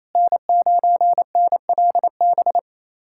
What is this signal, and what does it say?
Morse code, heard as a keyed tone, about 28 wpm, 700 Hz. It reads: N9NLB
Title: USA Callsign DE Callsign 28wpm